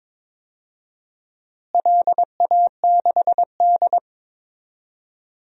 Morse code, heard as a keyed tone, about 22 wpm, 700 Hz. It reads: LA6D
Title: DXCC-100 Call Signs - 4 Characters - Repeated 3x - 22wpm